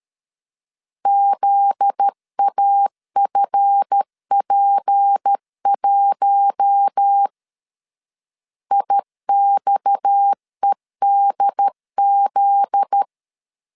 Morse code, heard as a keyed tone, about 13 wpm, 800 Hz. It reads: ZAFP1 IXEDZ
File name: LEZIONI / 7 GF9Z / 7 GF9Z 12wpm 2147